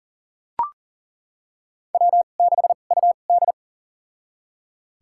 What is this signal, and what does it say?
Morse code, heard as a keyed tone, about 40 wpm, spaced as for 29 wpm, 700 Hz. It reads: W6UD
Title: USA Call Signs - 4 Characters - 2x Character Spacing - 40wpm